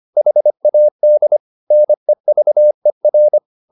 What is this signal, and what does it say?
Morse code, heard as a keyed tone, about 25 wpm, 600 Hz. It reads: HAD NEVER